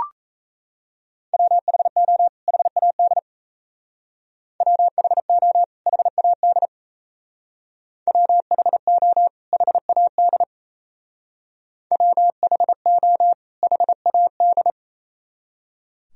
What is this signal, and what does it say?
Morse code, heard as a keyed tone, about 34 wpm, 700 Hz. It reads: WHO HAD WHO HAD WHO HAD WHO HAD
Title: Speed-Racing - Sets of 2 Words 28wpm